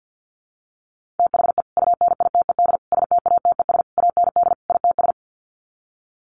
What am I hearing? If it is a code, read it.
Morse code, heard as a keyed tone, about 50 wpm, 700 Hz. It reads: THE UNITED STATES AND ITS
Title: Sets of 5 Words Encore 50wpm